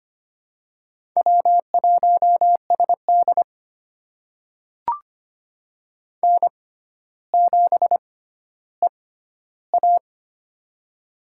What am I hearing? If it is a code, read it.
Morse code, heard as a keyed tone, about 25 wpm, spaced as for 13 wpm, 700 Hz. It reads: W1SD N7EA
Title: USA Call Signs - 4 Characters - 6x Character Spacing - 25wpm